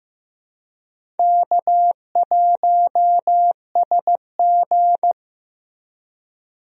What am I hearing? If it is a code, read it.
Morse code, heard as a keyed tone, about 16 wpm, 700 Hz. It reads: K1SG
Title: USA Call Signs - 4 Characters - Repeated 2x - 15wpm